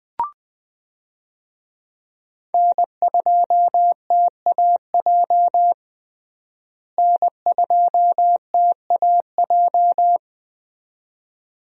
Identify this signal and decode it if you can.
Morse code, heard as a keyed tone, about 21 wpm, 700 Hz. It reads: N2TAJ N2TAJ
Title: USA Call Signs - 5 Characters - Repeated 2x - 20wpm